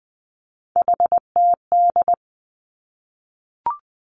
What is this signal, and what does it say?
Morse code, heard as a keyed tone, about 20 wpm, 700 Hz. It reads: HTD
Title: Three Challenging Letter-Number Combinations 20wpm